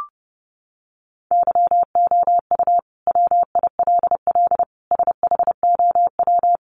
Morse code, heard as a keyed tone, about 30 wpm, 700 Hz. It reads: YOU WILL SHOW
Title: Sentences from Top 600 Words 30wpm